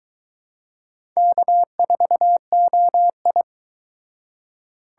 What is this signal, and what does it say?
Morse code, heard as a keyed tone, about 22 wpm, 700 Hz. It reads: K4OI